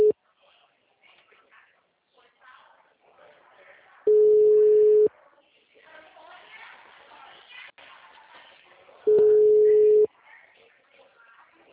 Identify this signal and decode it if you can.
Ringback tone (Europe, 425 Hz), with speech.
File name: out- -2016- - - .69541